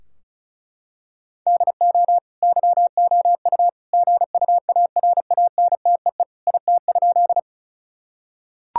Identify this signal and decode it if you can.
Morse code, heard as a keyed tone, about 36 wpm, 700 Hz. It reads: DO YOU GUARANTEE IT?
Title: Sentences from Top 1600 Words 35wpm